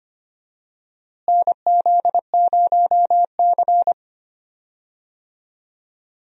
Morse code, heard as a keyed tone, about 25 wpm, 700 Hz. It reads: NZ0C
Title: USA Call Signs - 4 Characters - Repeated 2x - 25wpm